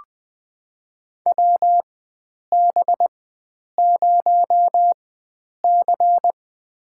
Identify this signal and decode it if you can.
Morse code, heard as a keyed tone, about 21 wpm, spaced as for 9 wpm, 700 Hz. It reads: WB0C
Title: USA Call Signs - 4 Characters - 4x Character Spacing - 20wpm